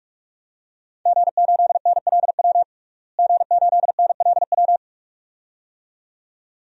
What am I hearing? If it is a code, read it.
Morse code, heard as a keyed tone, about 41 wpm, 700 Hz. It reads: G8NLW G8NLW
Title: DXCC-100 Call Signs - 5 Characters - Repeated 2x - 45wpm